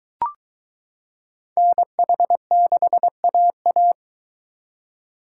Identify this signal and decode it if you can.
Morse code, heard as a keyed tone, about 23 wpm, 700 Hz. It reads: NH6AA